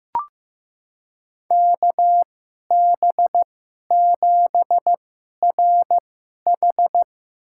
Morse code, heard as a keyed tone, about 15 wpm, spaced as for 11 wpm, 700 Hz. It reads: KB7RH
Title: USA Call Signs - 2x Character Spacing - 15wpm